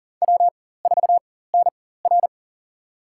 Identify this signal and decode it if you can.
Morse code, heard as a keyed tone, about 40 wpm, spaced as for 19 wpm, 700 Hz. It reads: W4NR
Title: USA Call Signs - 4x Character Spacing - 40wpm